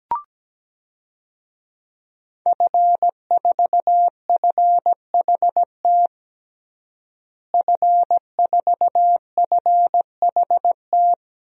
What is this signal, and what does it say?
Morse code, heard as a keyed tone, about 17 wpm, 700 Hz. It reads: F4FHT F4FHT
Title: DXCC-100 Call Signs - 5 Characters - Repeated 2x - 17wpm